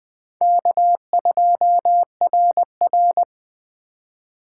Morse code, heard as a keyed tone, about 20 wpm, 700 Hz. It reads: K2RR
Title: Speed Racing - USA 1x2 Call Signs 17wpm